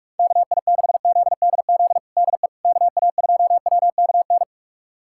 Morse code, heard as a keyed tone, about 41 wpm, 700 Hz. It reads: KI6ZDZ DE KA2WKN
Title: USA Callsign DE Callsign 45wpm